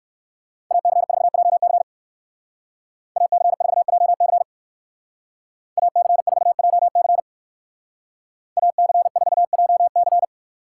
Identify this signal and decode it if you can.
Morse code, heard as a keyed tone, about 64 wpm, 700 Hz. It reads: AC4JC AC4JC AC4JC AC4JC